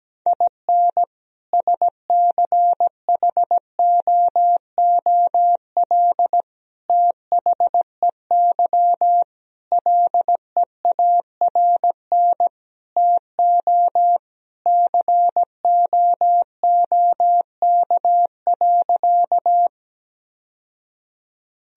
Morse code, heard as a keyed tone, about 17 wpm, 700 Hz. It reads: IN SCHOOL THEY LEARN TO COOK.
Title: Sentences from Top 1000 Words - Review Part 4 - 17wpm